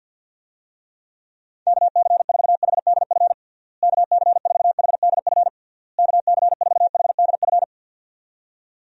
Morse code, heard as a keyed tone, about 51 wpm, 700 Hz. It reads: KC4HDF KC4HDF KC4HDF
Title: USA Call Signs - 6 Characters - Repeated 3x - 50wpm